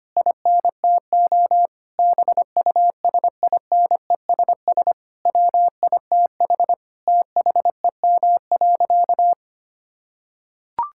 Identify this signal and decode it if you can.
Morse code, heard as a keyed tone, about 25 wpm, 700 Hz. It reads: INTO BUSINESS WITH THEM.
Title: Sentences from Top 1000 Words - Review Part 6 - 25wpm